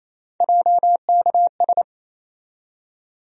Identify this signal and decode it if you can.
Morse code, heard as a keyed tone, about 28 wpm, 700 Hz. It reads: JKS
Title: USA Call Sign Postfixes - 3 Letters 28wpm